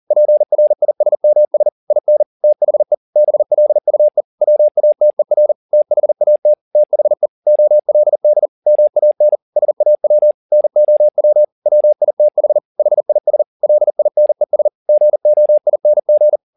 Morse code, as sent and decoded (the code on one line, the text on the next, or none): .--. .-. .. ... -- ... / .. -. / - .... / -... .-.. ..- . / .-- .- - . .-. / - .... .- - / - .... / --- .-.. -.. / -- .- -. / ... .- .-- / -. --- .-- / .-- .. - .... / .... .. ... / .-.. .. -. ... / --. --- .. -. --.
PRISMS IN THE BLUE WATER THAT THE OLD MAN SAW NOW WITH HIS LINES GOING